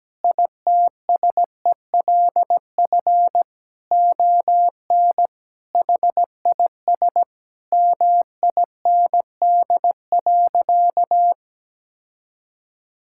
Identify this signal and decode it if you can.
Morse code, heard as a keyed tone, about 17 wpm, 700 Hz. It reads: ITSELF ON HIS MIND.
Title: Sentences from Top 1600 Words 17wpm